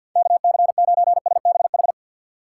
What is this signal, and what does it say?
Morse code, heard as a keyed tone, about 48 wpm, 700 Hz. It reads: KC9SBH
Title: USA Call Signs - 6 Characters - 4x Character Spacing - 50wpm